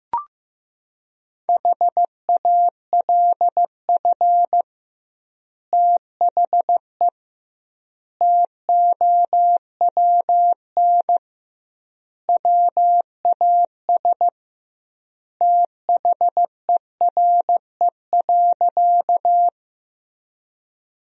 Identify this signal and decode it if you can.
Morse code, heard as a keyed tone, about 16 wpm, 700 Hz. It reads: HALF THE TOWN WAS THERE.
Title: 2x Word-Spacing - Sentences from Top 500 Words - 15wpm